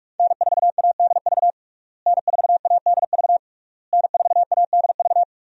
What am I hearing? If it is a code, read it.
Morse code, heard as a keyed tone, about 46 wpm, 700 Hz. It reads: N4ADV N4ADV N4ADV
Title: USA Call Signs - 5 Characters - Repeated 3x - 45wpm